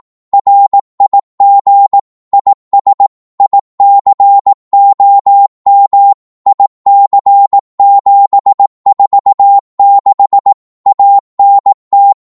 Morse code, heard as a keyed tone, about 18 wpm, 800 Hz. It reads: RIG IS ICOM IC746 ANT
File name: Morse / qso-18wpm